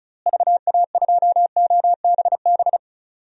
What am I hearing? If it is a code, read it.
Morse code, heard as a keyed tone, about 36 wpm, 700 Hz. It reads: VA2OBB
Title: DXCC-100 Call Signs - 6 Characters - Repeated 3x - 35wpm